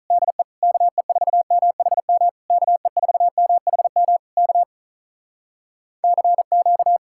Morse code, heard as a keyed tone, about 47 wpm, 700 Hz. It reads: DE KE4MHM KE4MHM K CQ